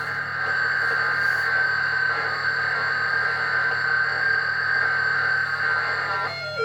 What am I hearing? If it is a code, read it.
FT8.